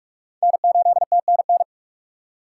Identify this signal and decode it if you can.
Morse code, heard as a keyed tone, about 46 wpm, 700 Hz. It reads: N8TNN
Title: USA Call Signs - 6x Character Spacing - 45wpm